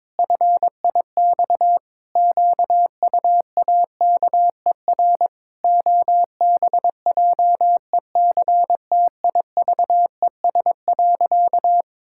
Morse code, heard as a keyed tone, about 22 wpm, 700 Hz. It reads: FIX QUAKER OBJECTIVES.
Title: Pangram Challenge v2 - 22wpm